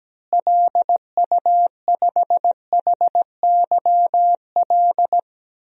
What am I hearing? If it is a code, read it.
Morse code, heard as a keyed tone, about 17 wpm, 700 Hz. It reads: LU5HYL
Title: DXCC-100 Call Signs - 6 Characters - Repeated 2x - 17wpm